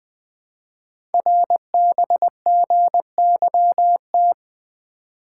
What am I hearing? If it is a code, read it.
Morse code, heard as a keyed tone, about 20 wpm, 700 Hz. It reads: RBGYT
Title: Code Groups - 5 Letters - 20wpm